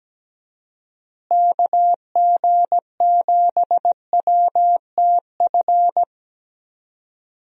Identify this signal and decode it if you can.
Morse code, heard as a keyed tone, about 17 wpm, 700 Hz. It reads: KG7WTF
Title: USA Call Signs - 6 Characters - Repeated 2x - 17wpm